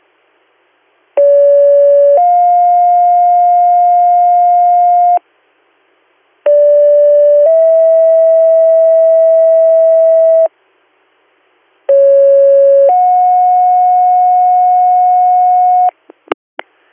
Two-tone page, heard as a one-second tone, then a three-second tone, about 570 then 710 Hz, about 580 then 650 Hz, about 550 then 730 Hz. It is heard from 1.2 to 15.9 s.